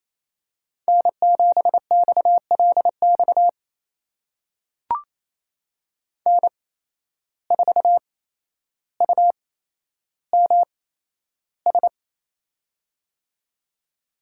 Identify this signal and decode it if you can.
Morse code, heard as a keyed tone, about 28 wpm, spaced as for 12 wpm, 700 Hz. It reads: N7XLX N4UMS
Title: USA Call Signs - 5 Characters - 8x Character Spacing - 28wpm